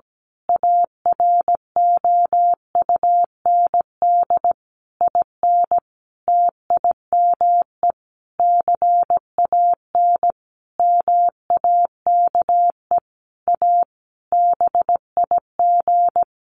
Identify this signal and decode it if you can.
Morse code, heard as a keyed tone, about 17 wpm, 700 Hz. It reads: AROUND IN TIME CAN MAKE A BIG
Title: A Sound Of Thunder 17wpm